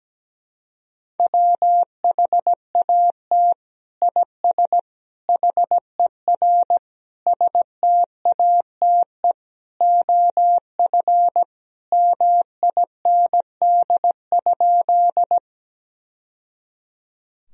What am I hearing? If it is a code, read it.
Morse code, heard as a keyed tone, about 17 wpm, 700 Hz. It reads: WHAT IS HER STATE OF MIND?
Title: Sentences from Top 1000 Words - Review Part 2 - 17wpm